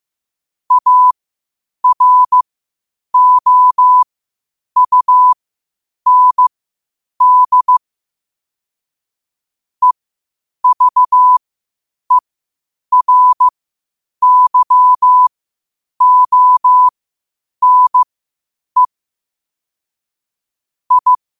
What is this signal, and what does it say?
Morse code, heard as a keyed tone, about 15 wpm, spaced as for 8 wpm, 1000 Hz. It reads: AROUND EVERYONE I
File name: qotdfiles / 8wpm